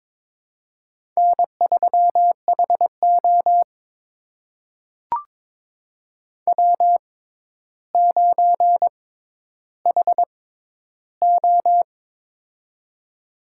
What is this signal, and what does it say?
Morse code, heard as a keyed tone, about 22 wpm, spaced as for 11 wpm, 700 Hz. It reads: N3HO W9HO
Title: USA Call Signs - 4 Characters - 6x Character Spacing - 22wpm